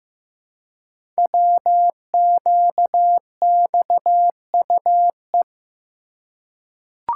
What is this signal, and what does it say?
Morse code, heard as a keyed tone, about 15 wpm, 700 Hz. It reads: WQXUE
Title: Code Groups - 5 Letters - 15wpm